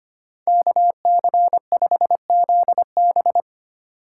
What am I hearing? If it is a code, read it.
Morse code, heard as a keyed tone, about 25 wpm, 700 Hz. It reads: KC5ZB
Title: USA Call Signs - 5 Characters - 4x Character Spacing - 25wpm